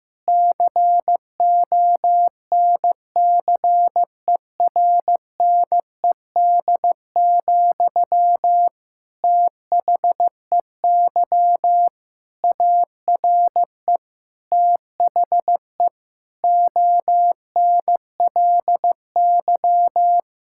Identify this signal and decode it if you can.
Morse code, heard as a keyed tone, about 15 wpm, 700 Hz. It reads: CONCERNED, THEY ARE THE ONLY
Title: Sentences from Top 1000 Words - Review Part 6 - 15wpm